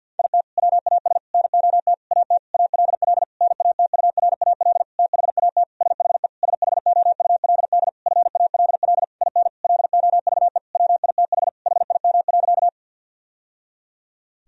Morse code, heard as a keyed tone, about 50 wpm, 700 Hz. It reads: IT WAS NOT AT ALL NATURAL THAT SHE SHOULD FALL IN LOVE WITH HIM.
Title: Sentences from Top 1000 Words - Review Part 4 - 50wpm